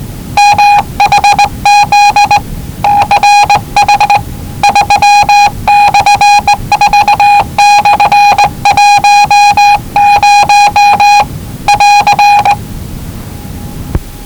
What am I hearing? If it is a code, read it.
Morse code, heard as a keyed tone, about 20 wpm, 840 Hz. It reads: MHZ CH 3/4/10 <AR>
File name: WellingtonRadioZlwCloseDownOnHfAndMf / 01-Zlw 4277khz 417.5 qrt until 0925z